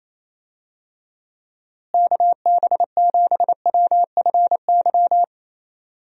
Morse code, heard as a keyed tone, about 28 wpm, 700 Hz. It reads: KB7WFY
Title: USA Call Signs - 6 Characters - Repeated 2x - 28wpm